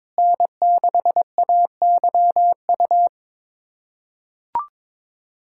Morse code, heard as a keyed tone, about 22 wpm, 700 Hz. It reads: N6AYU
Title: USA Callsign DE Callsign - 5 Characters - 22wpm